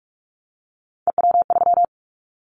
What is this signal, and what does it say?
Morse code, heard as a keyed tone, about 46 wpm, 700 Hz. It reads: EW3
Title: Callsign Prefixes for Top 100 DXCC - 45wpm